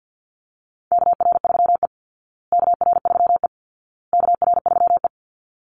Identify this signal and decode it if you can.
Morse code, heard as a keyed tone, about 50 wpm, 700 Hz. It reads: KR3E KR3E KR3E
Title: Call Signs Repeated 2x - 50wpm